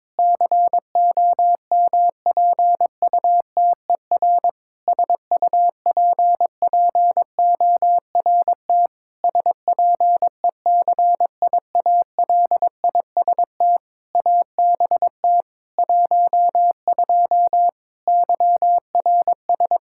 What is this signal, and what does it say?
Morse code, heard as a keyed tone, about 22 wpm, 700 Hz. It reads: COMPUTER SUPPORT SPECIALIST ABT 12 YRS
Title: Practice QSO - Exchange 3 - More Info - 22wpm